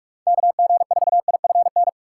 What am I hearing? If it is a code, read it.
Morse code, heard as a keyed tone, about 46 wpm, 700 Hz. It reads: KG4IFN
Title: USA Call Signs - 4x Character Spacing - 45wpm